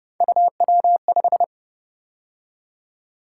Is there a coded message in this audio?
Yes: Morse code from 0.2 to 1.4 s, heard as a keyed tone, about 29 wpm, 700 Hz. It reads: UW5